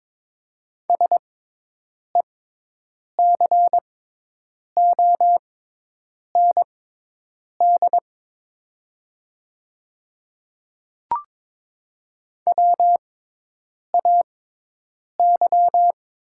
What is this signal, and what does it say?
Morse code, heard as a keyed tone, about 22 wpm, spaced as for 8 wpm, 700 Hz. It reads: SECOND WAY
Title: 6x Character Spacing - Top 300 Words - 22wpm